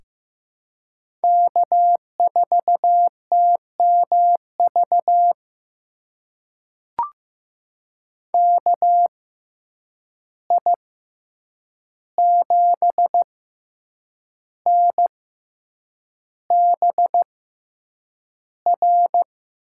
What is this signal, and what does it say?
Morse code, heard as a keyed tone, about 15 wpm, spaced as for 7 wpm, 700 Hz. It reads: K4TMV KI7NBR
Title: USA Call Signs - 6x Character Spacing - 15wpm